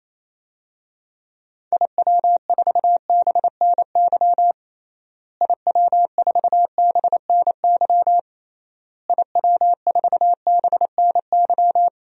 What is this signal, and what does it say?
Morse code, heard as a keyed tone, about 27 wpm, 700 Hz. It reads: IW4BNY IW4BNY IW4BNY
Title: DXCC-100 Call Signs - 6 Characters - Repeated 3x - 28wpm